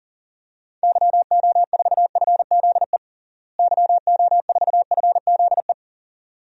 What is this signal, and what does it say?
Morse code, heard as a keyed tone, about 37 wpm, 700 Hz. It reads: YO4FZE YO4FZE
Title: DXCC-100 Call Signs - 6 Characters - Repeated 2x - 40wpm